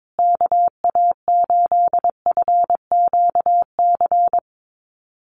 Morse code, heard as a keyed tone, about 22 wpm, 700 Hz. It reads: KA8FQC
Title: USA Call Signs Repeated 3x 22wpm